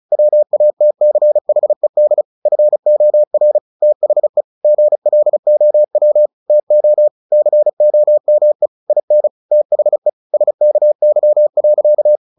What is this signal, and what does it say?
Morse code, heard as a keyed tone, about 34 wpm, 600 Hz. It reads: WATCHED FOR THE GLOW TO COME IN THE SKY.
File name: the old man and the sea 2 35wpm